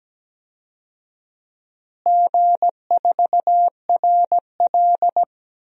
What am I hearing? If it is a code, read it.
Morse code, heard as a keyed tone, about 17 wpm, 700 Hz. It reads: G4RL
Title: DXCC-100 Call Signs - 4 Characters - Repeated 3x - 17wpm